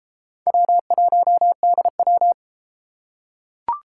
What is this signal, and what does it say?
Morse code, heard as a keyed tone, about 32 wpm, 700 Hz. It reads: W1DW